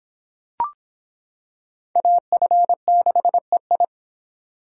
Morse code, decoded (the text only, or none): AF6EI